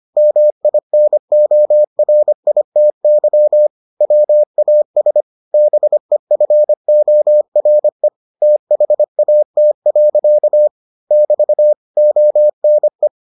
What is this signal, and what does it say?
Morse code, heard as a keyed tone, about 25 wpm, 600 Hz. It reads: MINORITY WAS BEFORE THAT. <BT> ONE